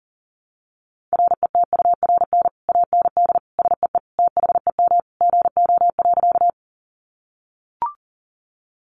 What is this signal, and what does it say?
Morse code, heard as a keyed tone, about 40 wpm, 700 Hz. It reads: RETURN AND SEE THEM GO.
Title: Sentences from Top 800 Words 40wpm